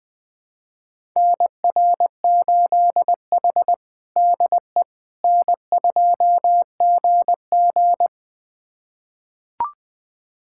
Morse code, heard as a keyed tone, about 21 wpm, 700 Hz. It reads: NR8H DE N2GG
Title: USA Callsign DE Callsign - 4 Characters - 20wpm